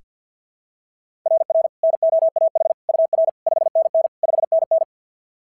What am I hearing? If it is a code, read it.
Morse code, heard as a keyed tone, about 46 wpm, 660 Hz. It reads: RR NOAH UR 5NN 5NN
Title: SOTA QSO Practice 50wpm